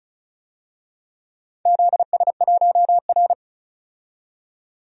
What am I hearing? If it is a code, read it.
Morse code, heard as a keyed tone, about 36 wpm, 700 Hz. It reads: ZS1R
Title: DXCC-100 Call Signs - 4 Characters - 35wpm